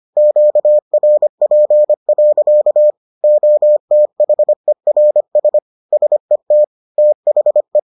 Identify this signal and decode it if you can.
Morse code, heard as a keyed tone, about 25 wpm, 600 Hz. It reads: QRP. OTHERS SET THE